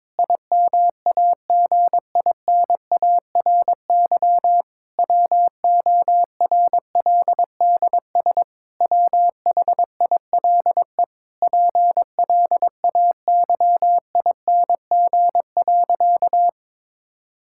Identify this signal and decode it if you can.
Morse code, heard as a keyed tone, about 22 wpm, 700 Hz. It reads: IMAGINARY WORLDS WHILE PLAYING.
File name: Difficult Sentences with Hints - 22wpm